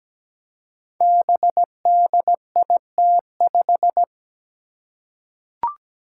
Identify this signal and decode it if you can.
Morse code, heard as a keyed tone, about 17 wpm, 700 Hz. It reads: BDIT5
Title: Five Challenging Letter-Number Combinations 17wpm